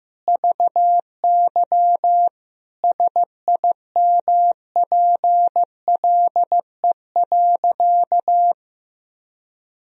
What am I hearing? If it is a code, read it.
Morse code, heard as a keyed tone, about 15 wpm, 700 Hz. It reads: VY SIMPLE.
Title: Sentences from Top 400 Words and Abbreviations 15wpm